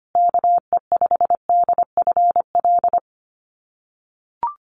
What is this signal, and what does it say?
Morse code, heard as a keyed tone, about 25 wpm, 700 Hz. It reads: KE5DFL
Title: USA Call Signs Repeated 3x 25wpm